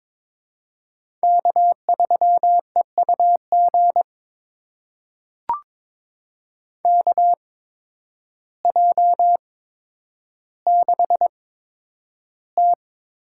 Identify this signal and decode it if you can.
Morse code, heard as a keyed tone, about 22 wpm, spaced as for 10 wpm, 700 Hz. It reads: K3EUG KJ6T
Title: USA Call Signs - 5 Characters - 8x Character Spacing - 22wpm